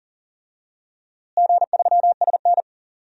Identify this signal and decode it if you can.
Morse code, heard as a keyed tone, about 40 wpm, 700 Hz. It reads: G3SN
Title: DXCC-100 Call Signs - 4 Characters - Repeated 2x - 40wpm